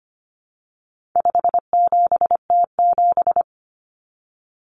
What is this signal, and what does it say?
Morse code, heard as a keyed tone, about 25 wpm, 700 Hz. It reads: 57T7
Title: Common Cut Numbers - 4-Digit 25wpm